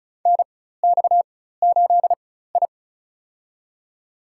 Morse code, heard as a keyed tone, about 36 wpm, spaced as for 16 wpm, 700 Hz. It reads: NX8I
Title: USA Call Signs - 4 Characters - 4x Character Spacing - 35wpm